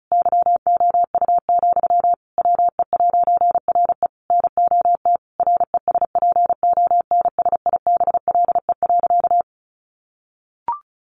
Morse code, heard as a keyed tone, about 34 wpm, 700 Hz. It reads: YOU, WE'RE NOT RESPONSIBLE.
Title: A Sound Of Thunder 35wpm